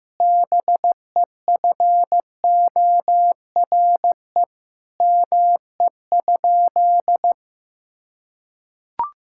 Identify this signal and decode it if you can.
Morse code, heard as a keyed tone, about 16 wpm, 700 Hz. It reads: BEFORE ME?
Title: Sentences from Top 1000 Words - Review Part 5 - 15wpm